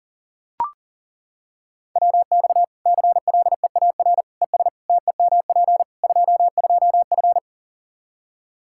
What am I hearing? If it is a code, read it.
Morse code, heard as a keyed tone, about 40 wpm, 700 Hz. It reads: WX CLEAR ES TEMP 22F
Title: Practice QSO - Exchange 2 - Info - 40wpm